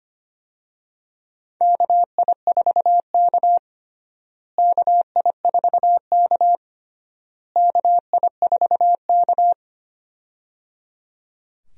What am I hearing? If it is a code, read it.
Morse code, heard as a keyed tone, about 25 wpm, 700 Hz. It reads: KI4K KI4K KI4K
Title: USA Call Signs - 4 Characters - Repeated 3x - 25wpm